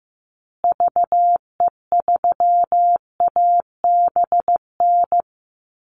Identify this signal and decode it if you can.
Morse code, heard as a keyed tone, about 15 wpm, 700 Hz. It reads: VE3ABN
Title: Call Signs Repeated 2x - 15wpm